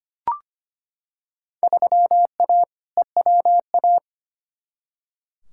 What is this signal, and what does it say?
Morse code, heard as a keyed tone, about 25 wpm, 700 Hz. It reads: 3A EWA
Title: Field Day Exchanges - 25wpm